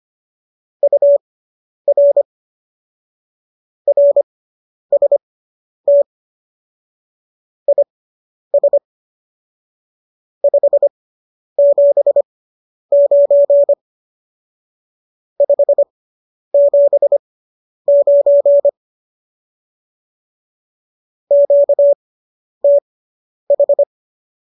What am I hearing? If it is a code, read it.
Morse code, heard as a keyed tone, about 25 wpm, spaced as for 10 wpm, 570 Hz. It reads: UR RST IS 579 579 QTH